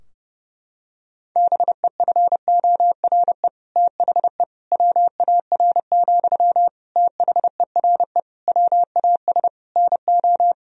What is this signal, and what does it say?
Morse code, heard as a keyed tone, about 30 wpm, 700 Hz. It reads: BEFORE THE WAR, THERE WAS NO
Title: Sentences from Top 1000 Words - Review Part 2 - 30wpm